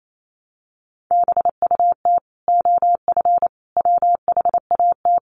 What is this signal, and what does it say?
Morse code, heard as a keyed tone, about 28 wpm, 700 Hz. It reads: BUT OF WHAT
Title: Sentences from Top 700 Words 28wpm